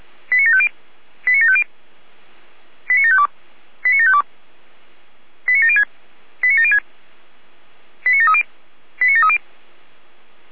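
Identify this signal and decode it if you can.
Five-tone selective call (ZVEI), heard as quick runs of five tones, tones of about 70 ms. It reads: 89750 89750 89752 89752 89706 89706 89730 89730
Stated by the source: ZVEI five-tone call received by SDR.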